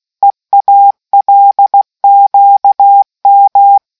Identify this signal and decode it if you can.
Morse code, heard as a keyed tone, about 16 wpm, 800 Hz. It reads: EALQM